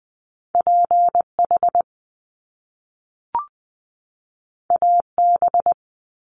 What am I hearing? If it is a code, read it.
Morse code, heard as a keyed tone, about 20 wpm, 700 Hz. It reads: P H A B
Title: Top Two Letter Combinations in Words 20wpm